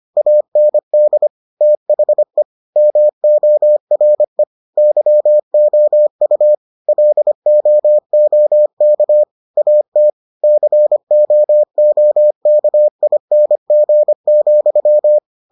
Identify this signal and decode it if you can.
Morse code, heard as a keyed tone, about 25 wpm, 600 Hz. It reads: AND THE MORE YOU LOOK AT COOKING,